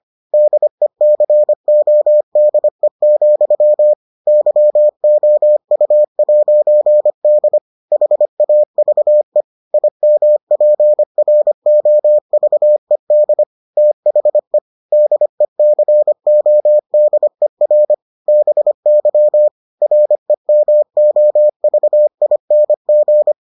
Morse code, heard as a keyed tone, about 25 wpm, 600 Hz. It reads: DECODE, YOU'D HAVE IMPROVED THE DECODER BY REMOVING